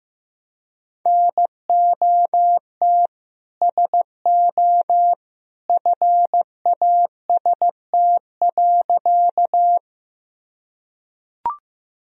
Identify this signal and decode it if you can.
Morse code, heard as a keyed tone, about 16 wpm, 700 Hz. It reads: NOT SO FAST.
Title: Sentences from Top 1000 Words - Review Part 2 - 15wpm